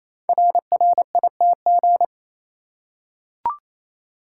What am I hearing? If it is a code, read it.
Morse code, heard as a keyed tone, about 29 wpm, 700 Hz. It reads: RRITG
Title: Code Groups - 5 Letter:Numbers - 28wpm